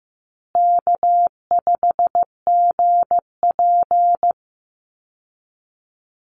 Morse code, heard as a keyed tone, about 15 wpm, 700 Hz. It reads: K5GP
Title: Easy Call Signs Repeated 3x - 15wpm